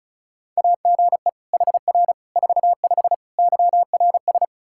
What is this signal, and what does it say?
Morse code, heard as a keyed tone, about 36 wpm, 700 Hz. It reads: AGE HR 45 YRS
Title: K1USN SST QSO Practice 35wpm